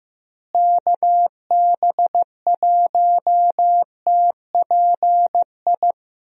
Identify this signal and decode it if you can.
Morse code, heard as a keyed tone, about 15 wpm, 700 Hz. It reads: KB1TPI
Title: USA Call Signs - 6 Characters - Repeated 3x - 15wpm